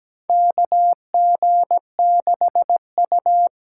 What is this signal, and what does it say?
Morse code, heard as a keyed tone, about 17 wpm, 700 Hz. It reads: KG6U
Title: USA Call Signs - 5 Characters - 8x Character Spacing - 17wpm